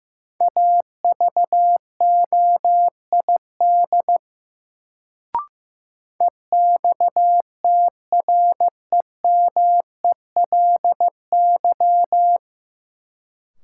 Morse code, heard as a keyed tone, about 15 wpm, 700 Hz. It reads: AVOID EXTREMELY
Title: Sentences from Top 1200 Words - New Word Drill Alt - 15wpm